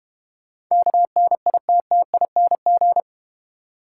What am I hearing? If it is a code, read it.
Morse code, heard as a keyed tone, about 30 wpm, 690 Hz. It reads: KNITTING
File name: Easy Words - Speed-Racing with Hints - 25wpm